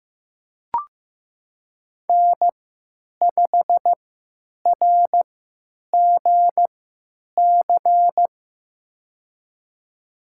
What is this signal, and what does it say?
Morse code, heard as a keyed tone, about 15 wpm, spaced as for 9 wpm, 700 Hz. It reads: N5RGC
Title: USA Call Signs - 5 Characters - 3x Character Spacing - 15wpm